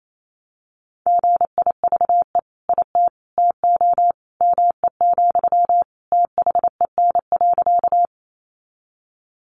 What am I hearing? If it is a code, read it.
Morse code, heard as a keyed tone, about 28 wpm, 700 Hz. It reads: GIVE IT TO ME, THEN.
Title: Sentences from Top 200 Words 28wpm